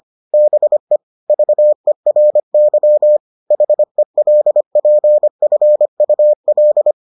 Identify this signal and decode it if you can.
Morse code, heard as a keyed tone, about 25 wpm, 600 Hz. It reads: BE VERY HELPFUL